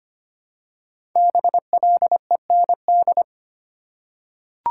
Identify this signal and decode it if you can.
Morse code, heard as a keyed tone, about 25 wpm, 700 Hz. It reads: BLEND
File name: 5 Letter Words - First 3 Letters Given - 25wpm